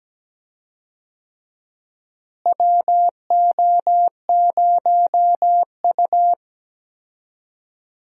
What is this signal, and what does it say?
Morse code, heard as a keyed tone, about 17 wpm, 700 Hz. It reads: WO0U
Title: USA Call Signs - 4 Characters - Repeated 3x - 17wpm